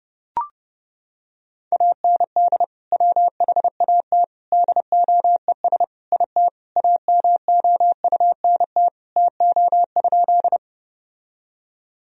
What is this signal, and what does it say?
Morse code, heard as a keyed tone, about 32 wpm, 700 Hz. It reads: AND WHAT DOES IT AMOUNT TO?
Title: Sentences from Top 1000 Words - Review Part 4 - 30wpm